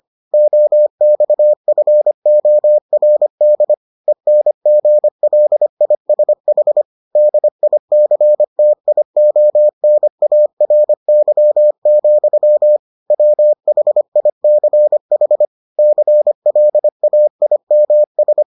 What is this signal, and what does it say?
Morse code, heard as a keyed tone, about 25 wpm, 600 Hz. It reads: OXFORD ENGLISH DICTIONARY, WHICH CLAIMS